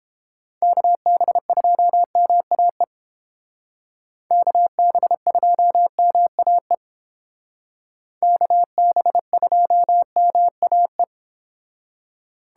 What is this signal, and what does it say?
Morse code, heard as a keyed tone, about 31 wpm, 700 Hz. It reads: KB2MAE KB2MAE KB2MAE